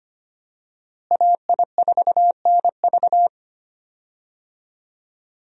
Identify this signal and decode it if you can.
Morse code, heard as a keyed tone, about 25 wpm, 700 Hz. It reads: AI4NV
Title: USA Call Signs - 5 Characters - Repeated 2x - 25wpm